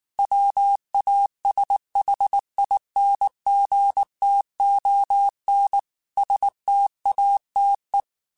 Morse code, heard as a keyed tone, about 19 wpm, 800 Hz. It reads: WASHINGTON STATE
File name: QSOs / QSOs-mp3s / WD0BGP 1@18